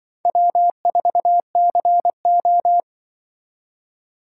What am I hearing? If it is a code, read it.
Morse code, heard as a keyed tone, about 24 wpm, 700 Hz. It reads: W4CO
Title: Speed Racing - USA 1x2 Call Signs 20wpm